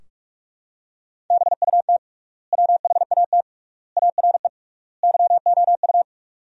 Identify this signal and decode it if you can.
Morse code, heard as a keyed tone, about 46 wpm, 700 Hz. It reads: BUT WHAT ARE YOU
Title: 3x Word-Spacing - Sentences from Top 100 Words - 45wpm